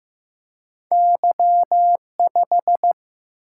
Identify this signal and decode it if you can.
Morse code, heard as a keyed tone, about 15 wpm, 700 Hz. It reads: Y5
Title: Single-Letter Single-Number 15wpm